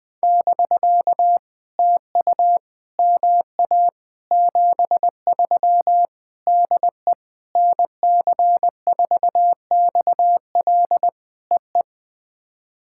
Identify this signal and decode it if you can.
Morse code, heard as a keyed tone, about 20 wpm, 700 Hz. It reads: <BK> TU MA 73 DE NC4XL EE
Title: POTA QSO Practice - 20wpm